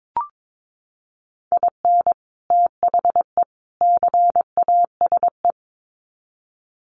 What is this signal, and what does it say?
Morse code, heard as a keyed tone, about 22 wpm, 700 Hz. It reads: IN THE CASE
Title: Sets of 3 Words 22wpm